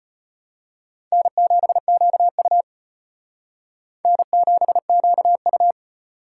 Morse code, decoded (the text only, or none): N7QU N7QU